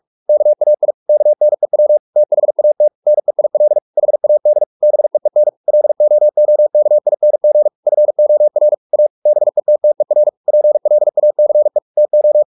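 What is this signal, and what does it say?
Morse code, heard as a keyed tone, about 44 wpm, 600 Hz. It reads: KAI KNEW THAT NEIL HAD BEEN LOOKING FOR A BETTER PLACE TO
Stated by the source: Morse practice at 45 wpm, the novel Never by Ken Follett in English, from CW med Gnister.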